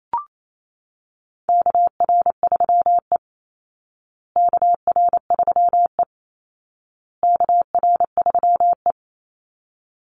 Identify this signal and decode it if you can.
Morse code, heard as a keyed tone, about 28 wpm, 700 Hz. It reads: KR3E KR3E KR3E
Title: Call Signs Repeated 2x - 28wpm